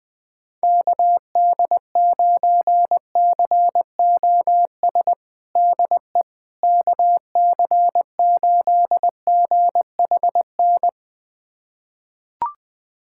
Morse code, heard as a keyed tone, about 20 wpm, 700 Hz. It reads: KD9COS DE KC8GHN
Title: USA Callsign DE Callsign - 6 Characters - 20wpm